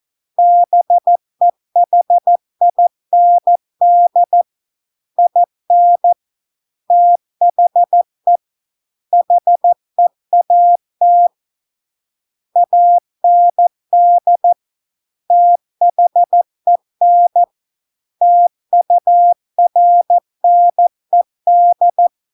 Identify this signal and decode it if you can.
Morse code, heard as a keyed tone, about 14 wpm, 700 Hz. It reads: BEHIND IN THE HEAT AND THEN TURNED